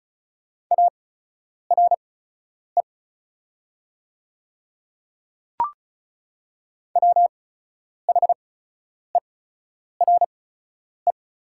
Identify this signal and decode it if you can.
Morse code, heard as a keyed tone, about 34 wpm, spaced as for 10 wpm, 690 Hz. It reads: ARE WHERE
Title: 8x Character Spacing - Top 200 Words - 35wpm